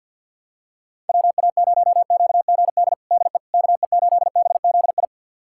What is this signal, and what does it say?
Morse code, heard as a keyed tone, about 47 wpm, 700 Hz. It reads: WA0QGD DE KE8BZI
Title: USA Callsign DE Callsign 50wpm